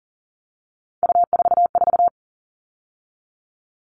Morse code, heard as a keyed tone, about 40 wpm, 700 Hz. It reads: U44
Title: All Cut Numbers - 3-Digit 40wpm